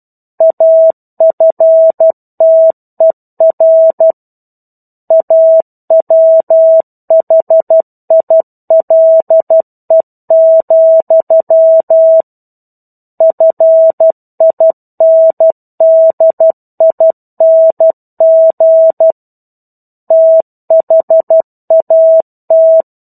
Morse code, heard as a keyed tone, about 12 wpm, 650 Hz. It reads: AFTER A WHILE, FINDING THAT
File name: 01-Chapter-01 12 WPM